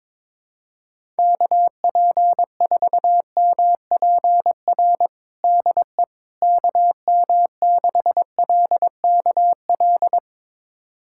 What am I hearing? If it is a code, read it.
Morse code, heard as a keyed tone, about 22 wpm, 700 Hz. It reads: KP4MPR DE KM6LKL
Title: USA Callsign DE Callsign - 6 Characters - 22wpm